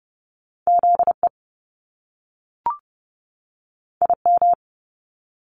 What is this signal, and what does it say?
Morse code, heard as a keyed tone, about 30 wpm, 700 Hz. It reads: ZE IM
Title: Top Two Letter Combinations in Words 30wpm